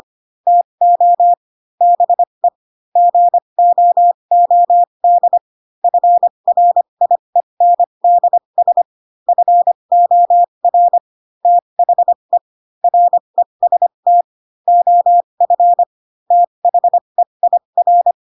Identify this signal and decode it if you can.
Morse code, heard as a keyed tone, about 25 wpm, spaced as for 22 wpm, 700 Hz. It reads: TO BE GOOD FRIENDS FOR THE REST OF THEIR
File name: Pinocchio / Pinocchio ch2 18 25 ik1hge